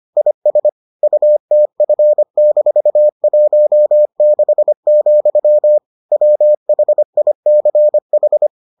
Morse code, heard as a keyed tone, about 25 wpm, 600 Hz. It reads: IS UTF-16, WHICH